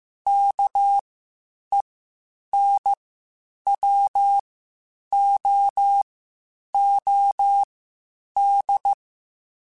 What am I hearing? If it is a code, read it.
Morse code, heard as a keyed tone, about 15 wpm, spaced as for 8 wpm, 800 Hz. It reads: KENWOOD
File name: (spaced) / QSOs Spaced-mp3s / WD0BGP 1 @13 sp